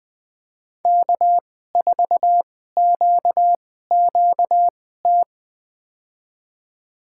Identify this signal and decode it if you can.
Morse code, heard as a keyed tone, about 20 wpm, spaced as for 14 wpm, 700 Hz. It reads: K4QQT
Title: USA Call Signs - 5 Characters - 2x Character Spacing - 20wpm